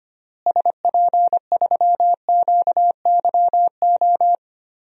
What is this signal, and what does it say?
Morse code, heard as a keyed tone, about 25 wpm, 700 Hz. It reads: SP3QYO
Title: DXCC-100 Call Signs - 6 Characters - Repeated 3x - 25wpm